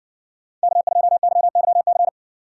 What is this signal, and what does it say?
Morse code, heard as a keyed tone, about 54 wpm, 700 Hz. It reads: K2YYC